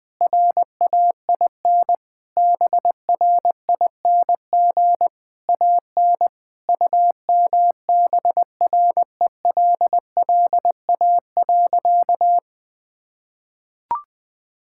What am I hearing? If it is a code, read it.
Morse code, heard as a keyed tone, about 20 wpm, 700 Hz. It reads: RAIN BRING AN UMBRELLA.
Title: Sentences with 3-Word Phrases - 20wpm